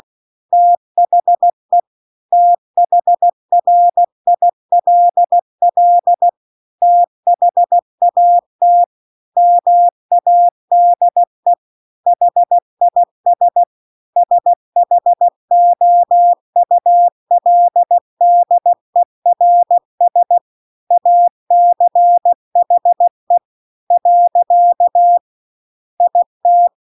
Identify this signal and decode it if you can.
Morse code, heard as a keyed tone, about 16 wpm, 700 Hz. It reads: THE THRILL THAT MADE HIS SHOULDERS ACHE . IT
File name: CW-The-Big-Twoheaded-River-2-16WPM